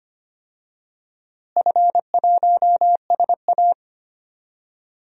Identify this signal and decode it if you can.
Morse code, heard as a keyed tone, about 25 wpm, 700 Hz. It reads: F1SA
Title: DXCC-100 Call Signs - 4 Characters - Repeated 2x - 25wpm